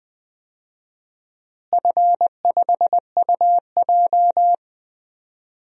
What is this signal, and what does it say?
Morse code, heard as a keyed tone, about 21 wpm, 700 Hz. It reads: F5UJ
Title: DXCC-100 Call Signs - 4 Characters - Repeated 2x - 20wpm